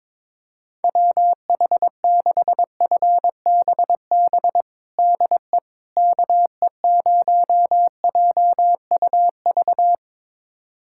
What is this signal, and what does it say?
Morse code, heard as a keyed tone, about 22 wpm, 700 Hz. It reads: WH6FBB DE KE0JUV
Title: USA Callsign DE Callsign 22wpm